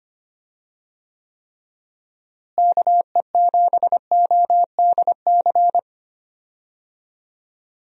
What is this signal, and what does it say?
Morse code, heard as a keyed tone, about 25 wpm, 700 Hz. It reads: KE7ODC
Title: USA Call Signs - 6 Characters - Repeated 3x - 25wpm